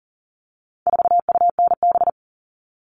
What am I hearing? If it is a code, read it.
Morse code, heard as a keyed tone, about 40 wpm, 700 Hz. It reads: 4UNB
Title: All Cut Numbers - 4-Digit 40wpm